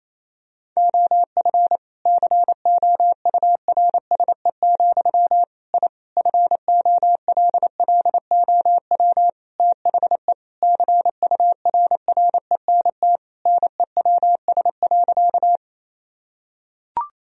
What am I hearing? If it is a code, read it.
Morse code, heard as a keyed tone, about 28 wpm, 700 Hz. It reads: OF COURSE, I FOLLOW THE CURRENT NEWS.
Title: Sentences from Top 1000 Words - Review Part 4 - 28wpm